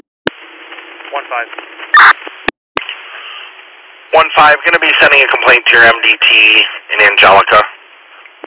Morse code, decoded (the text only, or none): none